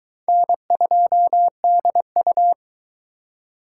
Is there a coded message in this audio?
Yes: Morse code from 0.3 to 2.5 s, heard as a keyed tone, about 22 wpm, 700 Hz. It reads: N2DU